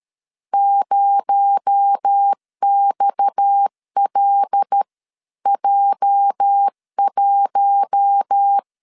Morse code, heard as a keyed tone, about 13 wpm, 800 Hz. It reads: 0XLJ1
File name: LEZIONI / 8 YJ45 / 8 YJ 12wpm 2235